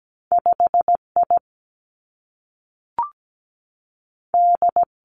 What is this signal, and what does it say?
Morse code, heard as a keyed tone, about 17 wpm, 700 Hz. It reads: 5I D
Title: Two Challenging Letter-Number Combinations 17wpm